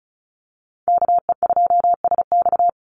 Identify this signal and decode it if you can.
Morse code, heard as a keyed tone, about 34 wpm, 700 Hz. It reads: KE2SX
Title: Call Signs - 35wpm